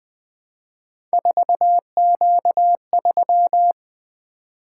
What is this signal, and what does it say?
Morse code, heard as a keyed tone, about 20 wpm, 700 Hz. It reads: 4Q3
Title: Three Letter-Number Combinations - 20wpm